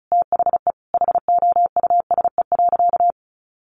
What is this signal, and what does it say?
Morse code, heard as a keyed tone, about 34 wpm, 700 Hz. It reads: THE HOUSE.
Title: Sentences from Top 500 Words 35wpm